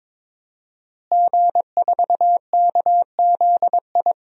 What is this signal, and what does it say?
Morse code, heard as a keyed tone, about 22 wpm, 700 Hz. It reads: G4KZI
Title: DXCC-100 Call Signs - 5 Characters - Repeated 2x - 22wpm